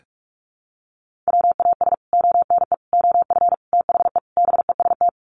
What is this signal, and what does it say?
Morse code, heard as a keyed tone, about 44 wpm, 700 Hz. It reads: WAS ONE OF THE BEST
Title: Sets of 5 Words Encore2 45wpm